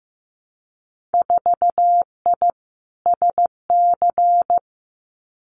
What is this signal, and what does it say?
Morse code, heard as a keyed tone, about 15 wpm, 700 Hz. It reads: 4I SC
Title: Winter Field Day Exchanges 15wpm